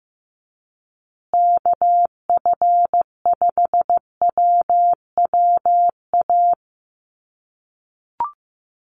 Morse code, heard as a keyed tone, about 15 wpm, 700 Hz. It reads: KF5WWA
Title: USA Call Signs Repeated 3x 15wpm